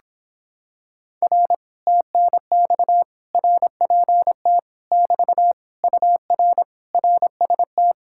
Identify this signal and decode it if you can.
Morse code, heard as a keyed tone, about 26 wpm, 700 Hz. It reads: R TNX RPT <BT> UR RST